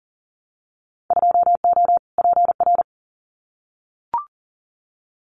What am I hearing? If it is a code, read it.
Morse code, heard as a keyed tone, about 40 wpm, 700 Hz. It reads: 2O PR
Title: Winter Field Day Exchanges Encore 40wpm